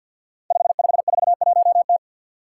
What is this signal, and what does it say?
Morse code, heard as a keyed tone, about 52 wpm, 700 Hz. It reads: 5541T
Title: Five Challenging Letter-Number Combinations 50wpm